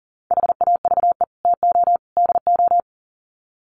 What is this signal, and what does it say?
Morse code, heard as a keyed tone, about 39 wpm, 700 Hz. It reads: HAVE TO DO
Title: Sets of 3 Words 40wpm